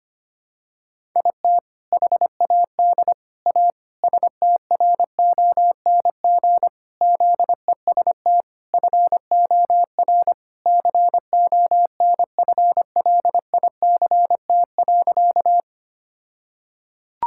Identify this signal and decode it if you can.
Morse code, heard as a keyed tone, about 25 wpm, 700 Hz. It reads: IT HAD A STRONG ZEST FOR CONFLICT.
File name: practice / 25wpm / All-Age Appropriate Jokes - 25wpm